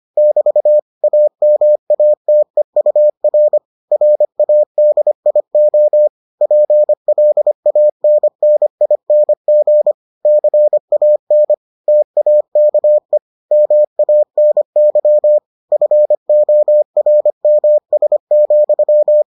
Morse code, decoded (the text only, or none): <BT> AMATEUR RADIO PLANNING CAN TAKE MANY FORMS,